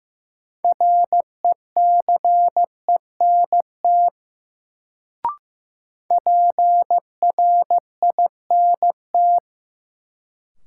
Morse code, heard as a keyed tone, about 15 wpm, 700 Hz. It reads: RECENT PRINT
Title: Sentences from Top 1400 Words - New Word Drill Alt - 15wpm